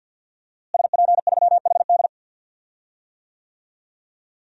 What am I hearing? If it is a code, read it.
Morse code, heard as a keyed tone, about 51 wpm, 700 Hz. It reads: SP3HD
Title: DXCC-100 Call Signs - 5 Characters - Repeated 3x - 50wpm